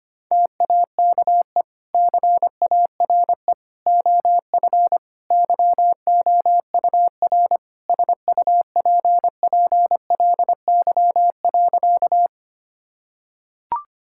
Morse code, heard as a keyed tone, about 25 wpm, 700 Hz. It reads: TAKE CARE OF YOUR SUPPLY.
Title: Sentences from Top 1000 Words - Review Part 5 - 25wpm